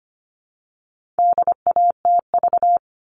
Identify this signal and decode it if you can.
Morse code, heard as a keyed tone, about 25 wpm, 700 Hz. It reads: DATV
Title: All Cut Numbers - 4-Digit 25wpm